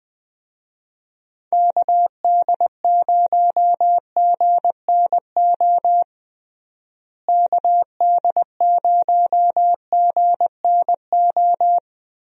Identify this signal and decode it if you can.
Morse code, heard as a keyed tone, about 20 wpm, 700 Hz. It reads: KD0GNO KD0GNO
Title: USA Call Signs - 6 Characters - Repeated 2x - 20wpm